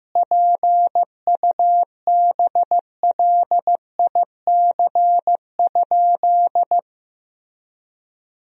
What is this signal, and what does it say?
Morse code, heard as a keyed tone, about 15 wpm, 700 Hz. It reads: PUBLIC?
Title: Sentences from Top 1000 Words - Review Part 3 - 15wpm